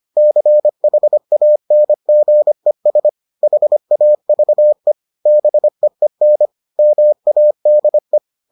Morse code, heard as a keyed tone, about 25 wpm, 600 Hz. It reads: CHANGES HAVE BEEN MADE